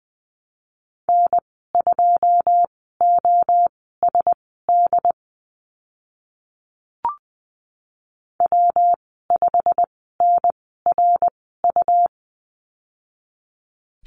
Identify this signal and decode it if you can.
Morse code, heard as a keyed tone, about 20 wpm, spaced as for 15 wpm, 700 Hz. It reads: N2OSD W5NRU
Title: 140 Lesson 23g - Letter f - Call Signs 20wpm